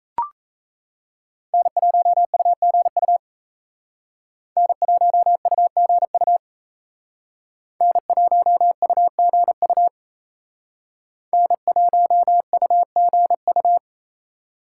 Morse code, heard as a keyed tone, about 35 wpm, 700 Hz. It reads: N1UGU N1UGU N1UGU N1UGU
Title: Speed Racing - USA 1x3 Call Signs 28wpm